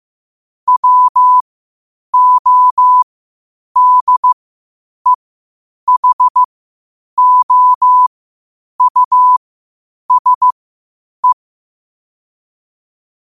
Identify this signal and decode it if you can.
Morse code, heard as a keyed tone, about 15 wpm, spaced as for 8 wpm, 1000 Hz. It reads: WODEHOUSE
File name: qotdfiles / 8wpm